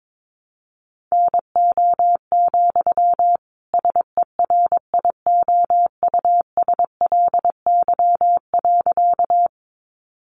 Morse code, heard as a keyed tone, about 22 wpm, 700 Hz. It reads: NO, SERIOUSLY.
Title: Sentences from Top 1200 Words 22wpm